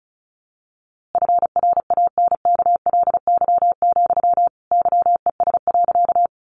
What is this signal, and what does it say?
Morse code, heard as a keyed tone, about 34 wpm, 700 Hz. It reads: FRANKLY, YES.
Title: A Sound Of Thunder 35wpm